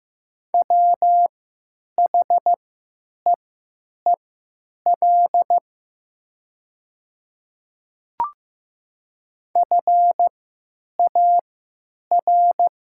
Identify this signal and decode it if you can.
Morse code, heard as a keyed tone, about 15 wpm, spaced as for 9 wpm, 700 Hz. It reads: WHEEL FAR
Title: 3x Character Spacing - Top 500 Words - 15wpm